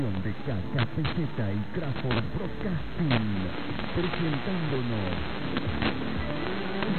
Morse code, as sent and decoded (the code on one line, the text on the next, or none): none